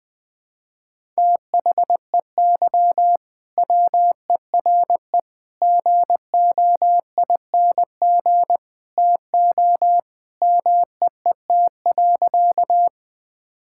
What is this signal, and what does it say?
Morse code, heard as a keyed tone, about 21 wpm, 700 Hz. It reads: THEY WERE GOING TO MEET.
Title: Sentences with 3-Word Phrases - 20wpm